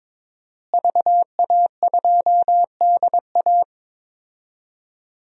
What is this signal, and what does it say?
Morse code, heard as a keyed tone, about 21 wpm, 700 Hz. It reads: VA2DA
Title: DXCC-100 Call Signs - 5 Characters - Repeated 3x - 22wpm